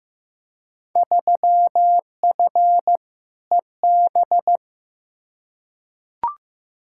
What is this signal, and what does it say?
Morse code, heard as a keyed tone, about 15 wpm, 700 Hz. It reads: 3F EB
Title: Field Day Exchanges - 15wpm